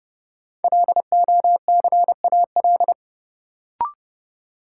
Morse code, heard as a keyed tone, about 32 wpm, 700 Hz. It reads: LOCAL
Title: Sentences from Top 1500 Words - New Word Drill Alt - 30wpm